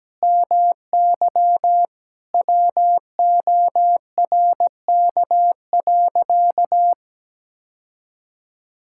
Morse code, heard as a keyed tone, about 17 wpm, 700 Hz. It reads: MY WORK.
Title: Sentences from Top 1000 Words - Review Part 6 - 17wpm